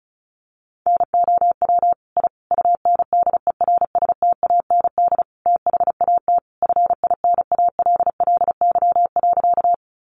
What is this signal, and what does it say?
Morse code, heard as a keyed tone, about 34 wpm, 700 Hz. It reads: NOW I UNDERSTAND THAT FINALLY.
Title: Sentences from Top 1000 Words 35wpm